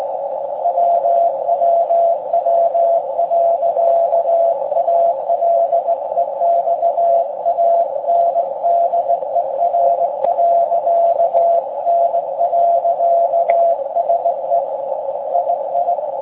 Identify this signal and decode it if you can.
Morse code, heard as a keyed tone, about 17 wpm, 680 Hz, with noise. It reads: WWW.ALEXANDER.N.SE II
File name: forum / im / aq grimeton